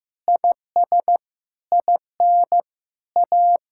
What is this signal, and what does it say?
Morse code, heard as a keyed tone, about 16 wpm, 700 Hz. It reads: IS IN A
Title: Sentences from Top 1000 Words - Review Part 6 - 15wpm